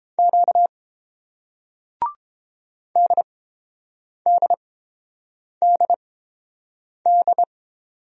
Morse code, heard as a keyed tone, about 30 wpm, spaced as for 7 wpm, 700 Hz. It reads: Q DDDD